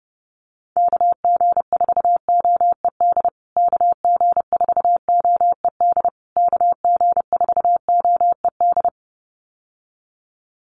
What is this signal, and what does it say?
Morse code, heard as a keyed tone, about 30 wpm, 700 Hz. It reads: KG4OED KG4OED KG4OED
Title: USA Call Signs Repeated 3x 30wpm